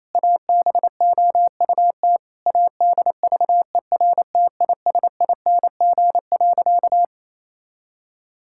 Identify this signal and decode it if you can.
Morse code, heard as a keyed tone, about 28 wpm, 700 Hz. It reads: ABOUT ADVERTISING.
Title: Sentences from Top 1000 Words - Review Part 6 - 28wpm